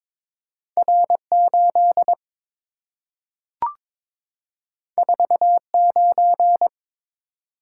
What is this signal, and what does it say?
Morse code, heard as a keyed tone, about 22 wpm, 700 Hz. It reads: R8 49
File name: practice / 22wpm / Two Letter-Number Combinations - 22wpm